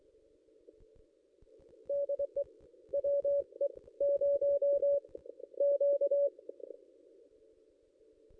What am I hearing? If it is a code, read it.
Morse code, heard as a keyed tone, about 23 wpm, 570 Hz. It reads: DE WE 0 Q
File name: contests / ss-cw-k5tr-op-n5rz -1900